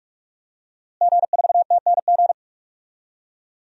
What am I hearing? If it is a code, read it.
Morse code, heard as a keyed tone, about 44 wpm, 700 Hz. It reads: G4TNG
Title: DXCC-100 Call Signs - 5 Characters - Repeated 2x - 45wpm